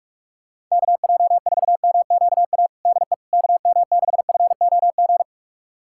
Morse code, heard as a keyed tone, about 45 wpm, 700 Hz. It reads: KJ4MQA DE KM6FOG
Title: USA Callsign DE Callsign - 6 Characters - 45wpm